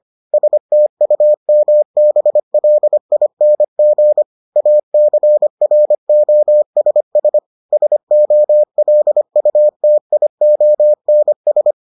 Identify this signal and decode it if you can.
Morse code, heard as a keyed tone, about 25 wpm, 600 Hz. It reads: STUMBLING ACROSS SOLUTIONS